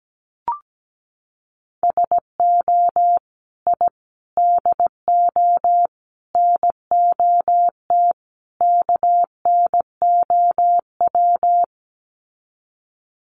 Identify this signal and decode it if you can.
Morse code, heard as a keyed tone, about 17 wpm, 700 Hz. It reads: SO I DO NOT KNOW
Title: Sets of 5 Words Encore 17wpm